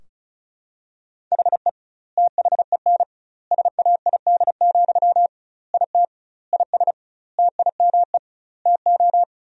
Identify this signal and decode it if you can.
Morse code, heard as a keyed tone, about 34 wpm, 700 Hz. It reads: HE THEN SAID, IT IS TIME TO
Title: 2x Word-Spacing - Sentences from Top 300 Words - 35wpm